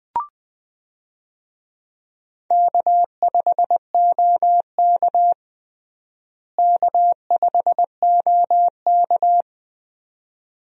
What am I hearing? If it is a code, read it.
Morse code, heard as a keyed tone, about 21 wpm, 700 Hz. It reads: K5OK K5OK
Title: USA Call Signs - 4 Characters - Repeated 2x - 20wpm